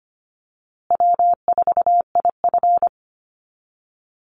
Morse code, heard as a keyed tone, about 25 wpm, 700 Hz. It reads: W4IF
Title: Easy Call Signs Repeated 3x - 25wpm